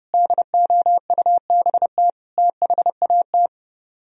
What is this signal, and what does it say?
Morse code, heard as a keyed tone, about 37 wpm, 700 Hz. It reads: DOUBT THAT
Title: Speed-Racing - Sets of 5 Words 30wpm